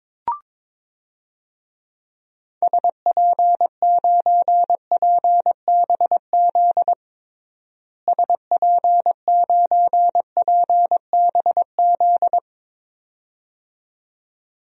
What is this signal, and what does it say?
Morse code, heard as a keyed tone, about 22 wpm, 700 Hz. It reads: SP9PBZ SP9PBZ
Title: DXCC-100 Call Signs - 6 Characters - Repeated 2x - 22wpm